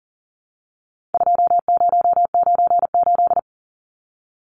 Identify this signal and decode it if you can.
Morse code, heard as a keyed tone, about 39 wpm, 700 Hz. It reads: 2098
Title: Numbers - 4-Digit 40wpm